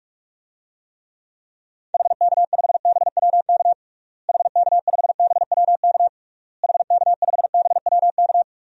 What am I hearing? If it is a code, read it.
Morse code, heard as a keyed tone, about 47 wpm, 700 Hz. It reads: HK5BWK HK5BWK HK5BWK
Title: DXCC-100 Call Signs - 6 Characters - Repeated 3x - 45wpm